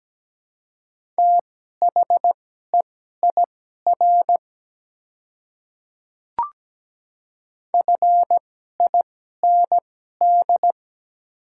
Morse code, heard as a keyed tone, about 17 wpm, spaced as for 12 wpm, 700 Hz. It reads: THEIR FIND
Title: 2x Character Spacing - Top 100 Words - 17wpm